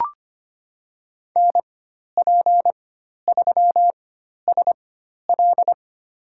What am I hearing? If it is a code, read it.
Morse code, heard as a keyed tone, about 25 wpm, spaced as for 12 wpm, 700 Hz. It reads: NP3SL
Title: USA Call Signs - 5 Characters - 4x Character Spacing - 25wpm